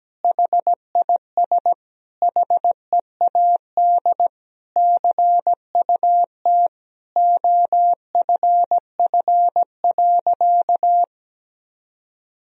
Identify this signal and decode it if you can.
Morse code, heard as a keyed tone, about 17 wpm, 700 Hz. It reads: HIS HEAD CUT OFF.
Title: Sentences from Top 1000 Words - Review Part 1 - 17wpm